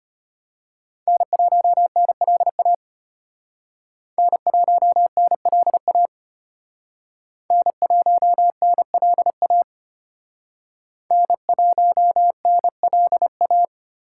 Morse code, heard as a keyed tone, about 33 wpm, 700 Hz. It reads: N1NLA N1NLA N1NLA N1NLA